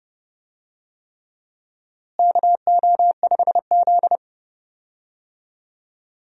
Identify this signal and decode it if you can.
Morse code, heard as a keyed tone, about 31 wpm, 700 Hz. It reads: KO5Z
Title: USA Call Signs - 4 Characters - Repeated 3x - 30wpm